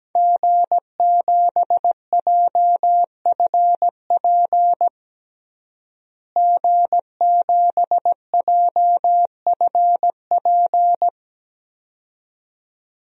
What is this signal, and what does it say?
Morse code, heard as a keyed tone, about 17 wpm, 700 Hz. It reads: G7JFP G7JFP
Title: DXCC-100 Call Signs - 5 Characters - Repeated 2x - 17wpm